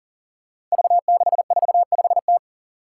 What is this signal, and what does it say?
Morse code, heard as a keyed tone, about 44 wpm, 700 Hz. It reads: V645T
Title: Five Challenging Letter-Number Combinations 40wpm